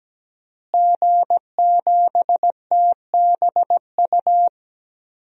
Morse code, heard as a keyed tone, about 17 wpm, 700 Hz. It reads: G7TBU
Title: DXCC-100 Call Signs - 5 Characters - Repeated 3x - 17wpm